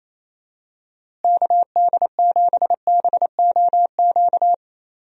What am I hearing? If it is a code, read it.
Morse code, heard as a keyed tone, about 27 wpm, 700 Hz. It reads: KD7BOQ
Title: USA Call Signs - 6 Characters - Repeated 3x - 28wpm